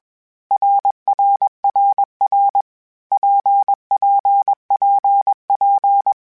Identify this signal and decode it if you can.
Morse code, heard as a keyed tone, about 21 wpm, 800 Hz. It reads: RRRR PPPP
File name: Groups@20 / Groups@20 mp3s / RKPX@20